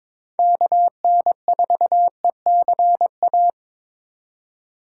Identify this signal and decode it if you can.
Morse code, heard as a keyed tone, about 25 wpm, 700 Hz. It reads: KN4ECA
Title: Speed Racing - USA 2x3 Call Signs 22wpm